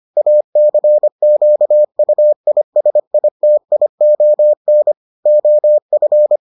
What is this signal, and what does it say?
Morse code, heard as a keyed tone, about 25 wpm, 600 Hz. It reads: ACQUISITION OF